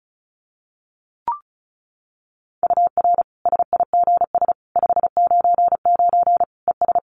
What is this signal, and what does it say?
Morse code, heard as a keyed tone, about 34 wpm, 700 Hz. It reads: UR SIGS 599 ES
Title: Common QSO Phrases 35wpm